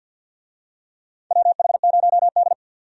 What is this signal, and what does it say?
Morse code, heard as a keyed tone, about 53 wpm, 700 Hz. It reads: WH0D
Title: USA Call Signs - 4 Characters - 50wpm